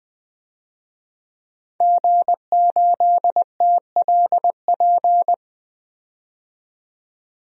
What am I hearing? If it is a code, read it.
Morse code, heard as a keyed tone, about 20 wpm, 700 Hz. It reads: G8TLP
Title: DXCC-100 Call Signs - 5 Characters - 20wpm